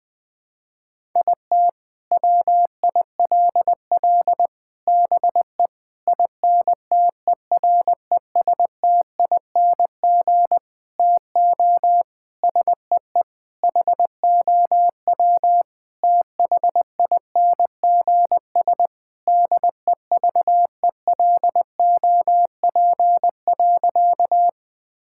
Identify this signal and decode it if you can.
Morse code, heard as a keyed tone, about 21 wpm, 700 Hz. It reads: IT WILL BE INTERESTING TO SEE HOW THINGS DEVELOP.
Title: Sentences with 6-Word Phrases - 20wpm